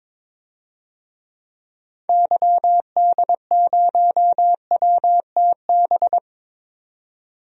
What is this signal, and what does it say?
Morse code, heard as a keyed tone, about 22 wpm, 700 Hz. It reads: YD0WTB
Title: DXCC-100 Call Signs - 6 Characters - 22wpm